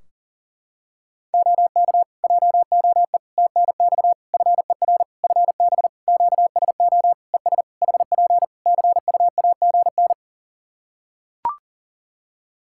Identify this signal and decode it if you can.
Morse code, heard as a keyed tone, about 40 wpm, 700 Hz. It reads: OK JOE TNX FER FB QSO ES HP CUAGN
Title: Practice QSO - Exchange 4 - Ending - 40wpm